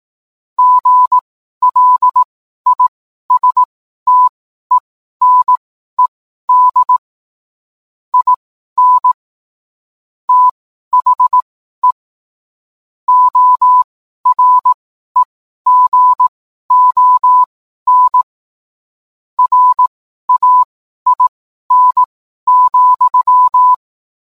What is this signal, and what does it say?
Morse code, heard as a keyed tone, about 18 wpm, spaced as for 12 wpm, 1000 Hz. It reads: GLISTENED IN THE OREGON RAIN,